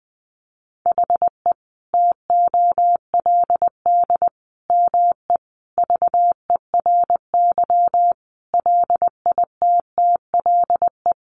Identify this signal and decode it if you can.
Morse code, heard as a keyed tone, about 20 wpm, 700 Hz. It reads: HE TOLD ME VERY LITTLE
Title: Sentences from Top 1500 Words 20wpm